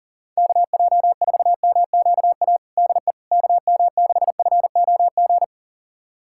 Morse code, heard as a keyed tone, about 40 wpm, 690 Hz. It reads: KJ4MQA DE KM6FOG
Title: USA Callsign DE Callsign - 6 Characters - 40wpm